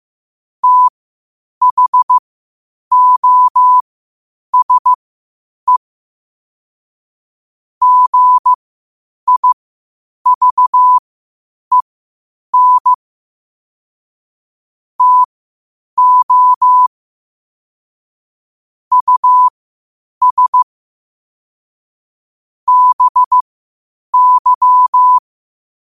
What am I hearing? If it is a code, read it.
Morse code, heard as a keyed tone, about 15 wpm, spaced as for 8 wpm, 1000 Hz. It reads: THOSE GIVEN TO US BY
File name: qotdfiles / 8wpm